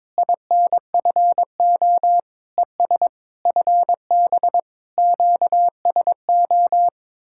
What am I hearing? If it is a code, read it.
Morse code, heard as a keyed tone, about 22 wpm, 700 Hz. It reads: INFO ES FB QSO
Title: Speed-Racing Common QSO Phrases Encore - 22wpm